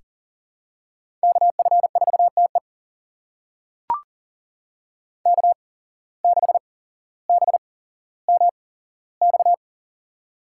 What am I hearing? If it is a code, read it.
Morse code, heard as a keyed tone, about 40 wpm, spaced as for 17 wpm, 700 Hz. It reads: KF4TE K6BMX
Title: USA Call Signs - 5 Characters - 8x Character Spacing - 40wpm